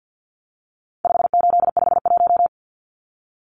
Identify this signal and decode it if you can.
Morse code, heard as a keyed tone, about 50 wpm, 700 Hz. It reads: 5851
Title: Numbers - 4-Digit 50wpm